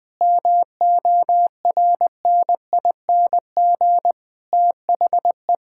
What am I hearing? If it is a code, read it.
Morse code, heard as a keyed tone, about 20 wpm, 700 Hz. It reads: MORNING THE
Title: Sentences with 7-Word Phrases - 20wpm